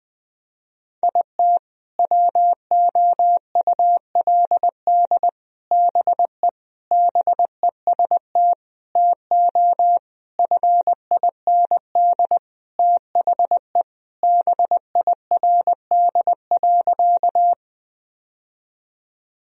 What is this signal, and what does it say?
Morse code, heard as a keyed tone, about 20 wpm, 700 Hz. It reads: IT WOULD BE BEST TO FIND THE BIRD.
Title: Sentences from Top 1000 Words - Review Part 2 - 20wpm